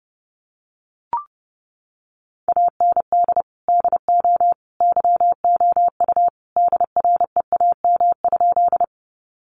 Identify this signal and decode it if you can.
Morse code, heard as a keyed tone, about 28 wpm, 700 Hz. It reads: AND DO YOU DREAM?
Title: Sentences from Top 1500 Words 30wpm